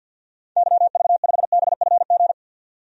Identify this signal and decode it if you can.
Morse code, heard as a keyed tone, about 50 wpm, 700 Hz. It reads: YV5BFG
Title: DXCC-100 Call Signs - 6 Characters - 50wpm